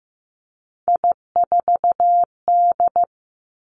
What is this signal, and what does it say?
Morse code, heard as a keyed tone, about 15 wpm, 700 Hz. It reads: I4D
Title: Three Challenging Letter-Number Combinations 15wpm